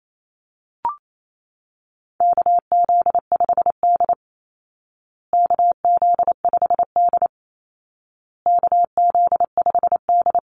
Morse code, heard as a keyed tone, about 28 wpm, 700 Hz. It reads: KZ5D KZ5D KZ5D
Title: Call Signs Repeated 2x - 28wpm